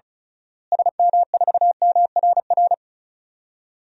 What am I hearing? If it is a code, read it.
Morse code, heard as a keyed tone, about 34 wpm, 700 Hz. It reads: SM4MRR
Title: DXCC-100 Call Signs - 6 Characters - Repeated 2x - 35wpm